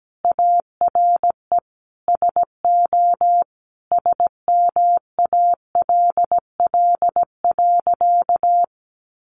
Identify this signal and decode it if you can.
Morse code, heard as a keyed tone, about 17 wpm, 700 Hz. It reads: ARE SO SMALL.
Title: Additional Easy Phrases 17wpm